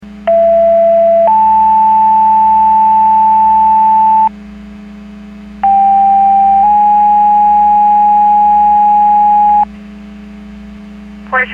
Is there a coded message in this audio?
Yes: two-tone page from 0.3 to 9.6 s, heard as a one-second tone, then a three-second tone, about 670 then 880 Hz, about 790 then 830 Hz, with speech.